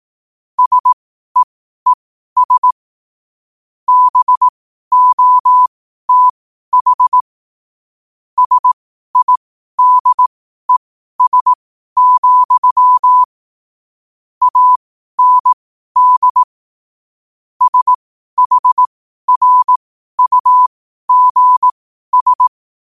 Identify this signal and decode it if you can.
Morse code, heard as a keyed tone, about 18 wpm, spaced as for 12 wpm, 1000 Hz. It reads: SEES BOTH SIDES, AND SHRUGS
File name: qotdfiles / 12wpm